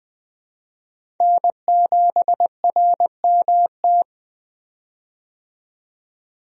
Morse code, heard as a keyed tone, about 20 wpm, 700 Hz. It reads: N7RMT
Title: USA Call Signs - 5 Characters - 20wpm